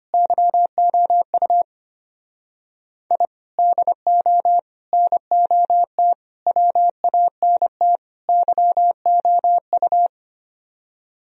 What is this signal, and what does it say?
Morse code, heard as a keyed tone, about 32 wpm, 700 Hz. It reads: YOU I DO NOT WANT YOU